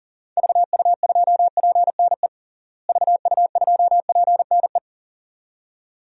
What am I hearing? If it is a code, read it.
Morse code, heard as a keyed tone, about 40 wpm, 700 Hz. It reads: VU2PNE VU2PNE
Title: DXCC-100 Call Signs - 6 Characters - Repeated 2x - 40wpm